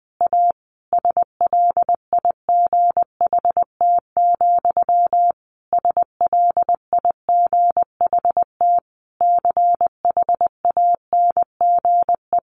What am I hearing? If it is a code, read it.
Morse code, heard as a keyed tone, about 20 wpm, 700 Hz. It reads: A SLIGHT, SLIGHT CHANGE
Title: A Sound Of Thunder 20wpm